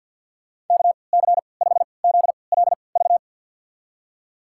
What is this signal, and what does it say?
Morse code, heard as a keyed tone, about 53 wpm, spaced as for 29 wpm, 700 Hz. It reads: KC5ZLV
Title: USA Call Signs - 6 Characters - 3x Character Spacing - 50wpm